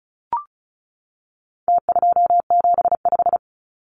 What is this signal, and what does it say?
Morse code, heard as a keyed tone, about 34 wpm, 700 Hz. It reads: T275
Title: Numbers - Contest Style - 4-Digit 35wpm